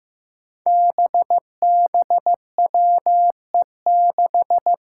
Morse code, heard as a keyed tone, about 15 wpm, 700 Hz. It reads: BBWE6
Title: Five Challenging Letter-Number Combinations 15wpm